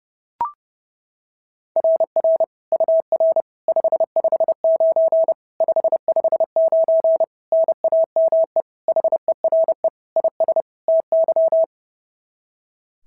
Morse code, heard as a keyed tone, about 30 wpm, 690 Hz. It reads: RR UR 559 559 NAME HERE IS TY
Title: SOTA QSO Practice 30wpm